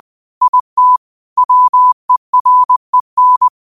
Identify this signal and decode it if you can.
Morse code, heard as a keyed tone, about 20 wpm, 1000 Hz. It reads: IT WEREN